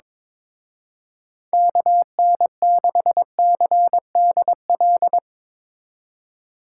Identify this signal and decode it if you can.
Morse code, heard as a keyed tone, about 24 wpm, 700 Hz. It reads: KN6CDL
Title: Speed Racing - USA 2x3 Call Signs 22wpm